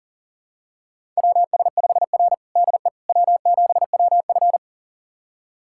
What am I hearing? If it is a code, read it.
Morse code, heard as a keyed tone, about 40 wpm, 700 Hz. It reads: WS5R DE W7WF
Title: USA Callsign DE Callsign - 4 Characters - 40wpm